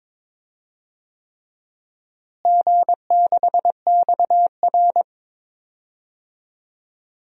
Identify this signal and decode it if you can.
Morse code, heard as a keyed tone, about 23 wpm, 700 Hz. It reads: G6XR
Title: DXCC-100 Call Signs - 4 Characters - Repeated 3x - 22wpm